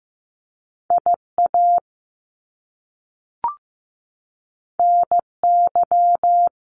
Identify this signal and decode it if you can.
Morse code, heard as a keyed tone, about 15 wpm, 700 Hz. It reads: IA NY
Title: US States 15wpm